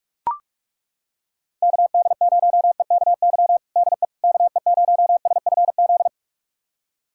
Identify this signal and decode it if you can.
Morse code, heard as a keyed tone, about 41 wpm, 700 Hz. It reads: KD0EKY DE KE0SFZ
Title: USA Callsign DE Callsign - 6 Characters - 45wpm